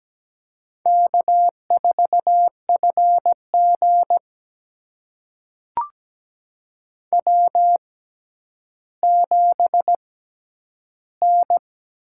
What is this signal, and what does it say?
Morse code, heard as a keyed tone, about 17 wpm, spaced as for 10 wpm, 700 Hz. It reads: K4FG W7N
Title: USA Call Signs - 4 Characters - 6x Character Spacing - 17wpm